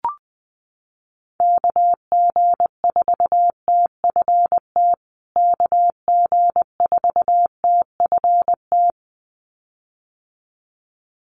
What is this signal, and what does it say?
Morse code, heard as a keyed tone, about 20 wpm, 700 Hz. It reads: KG4TFT KG4TFT
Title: USA Call Signs Repeated 2x 20wpm